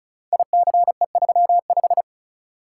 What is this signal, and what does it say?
Morse code, heard as a keyed tone, about 36 wpm, 700 Hz. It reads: ICE35
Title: Code Groups - 5 Letter:Numbers - 35wpm